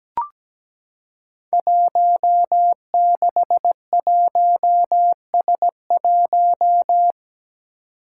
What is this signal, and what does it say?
Morse code, heard as a keyed tone, about 17 wpm, 700 Hz. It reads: 161S1
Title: Five Challenging Letter-Number Combinations 17wpm